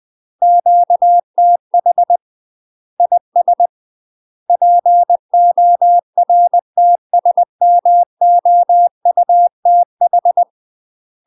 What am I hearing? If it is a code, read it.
Morse code, heard as a keyed tone, about 20 wpm, 700 Hz. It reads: QTH IS PORTSMOUTH
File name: UK MASTER QSOs TRAINING / 20WPM / UK MASTER QSOs 0046